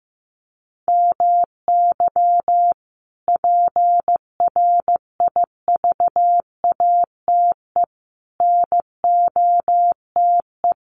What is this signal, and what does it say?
Morse code, heard as a keyed tone, about 15 wpm, 700 Hz. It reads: MY PRIVATE NOTE
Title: Sentences from Top 1000 Words 15wpm